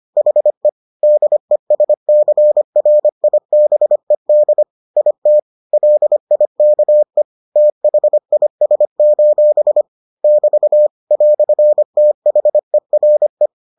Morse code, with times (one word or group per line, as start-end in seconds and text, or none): 0.2-0.7 HE
1.0-4.6 DESCRIBED
5.0-5.4 IT
5.7-7.2 LIKE
7.6-9.8 THIS:
10.2-10.9 <BT>
11.1-13.5 "THERE